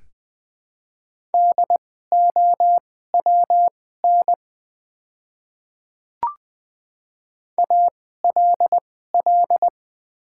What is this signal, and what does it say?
Morse code, heard as a keyed tone, about 20 wpm, spaced as for 15 wpm, 700 Hz. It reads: DOWN ALL
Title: 2x Character Spacing - Top 100 Words - 20wpm